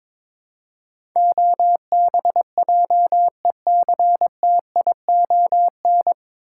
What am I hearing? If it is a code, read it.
Morse code, heard as a keyed tone, about 22 wpm, 700 Hz. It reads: OBJECTION
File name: 9 Letter Words - First 3 Letters Given - 22wpm